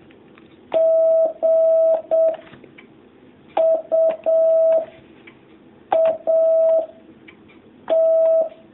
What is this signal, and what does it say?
Morse code, heard as a keyed tone, about 7 wpm, 650 Hz. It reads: G U A T
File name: jh8cbh / 2019cw